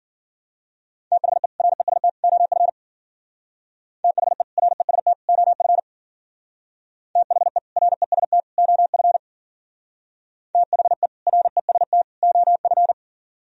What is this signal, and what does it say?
Morse code, heard as a keyed tone, about 52 wpm, 700 Hz. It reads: THE REST OF THE REST OF THE REST OF THE REST OF